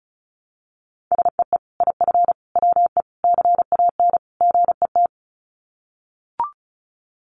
Morse code, heard as a keyed tone, about 34 wpm, 700 Hz. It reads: SEE IF WE CAN GET
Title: Sets of 5 Words Encore2 35wpm